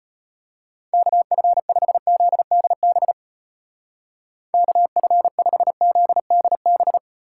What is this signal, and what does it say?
Morse code, heard as a keyed tone, about 31 wpm, 700 Hz. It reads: KF5ZDB KF5ZDB
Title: Speed Racing - USA 2x3 Call Signs 25wpm